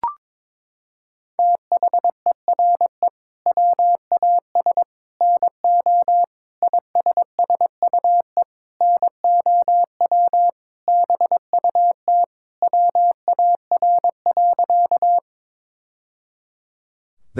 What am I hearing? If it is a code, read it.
Morse code, heard as a keyed tone, about 22 wpm, 700 Hz. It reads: THERE WAS NO ISSUE NOW BUT WAR.
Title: Sentences from Top 1000 Words - Review Part 6 - 22wpm